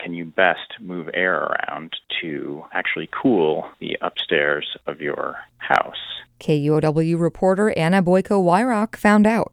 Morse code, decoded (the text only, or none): none